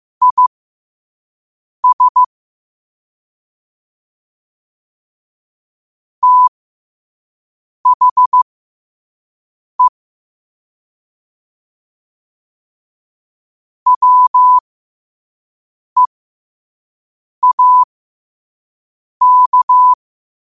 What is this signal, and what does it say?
Morse code, heard as a keyed tone, about 15 wpm, spaced as for 5 wpm, 1000 Hz. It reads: IS THE WEAK
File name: qotdfiles / 5wpm